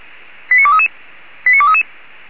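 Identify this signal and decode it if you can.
Five-tone selective call (ZVEI), heard as quick runs of five tones, tones of about 70 ms. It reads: 89233 89233
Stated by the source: ZVEI five-tone selective call received on an SDR.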